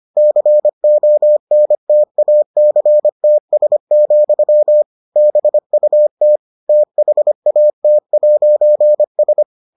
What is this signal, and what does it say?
Morse code, heard as a keyed tone, about 25 wpm, 600 Hz. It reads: CONTACTS, BUT THAT'S